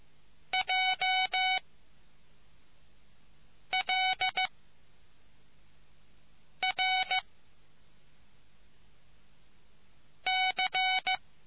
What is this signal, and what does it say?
Morse code, heard as a keyed tone, about 15 wpm, 2250 Hz. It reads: J L R C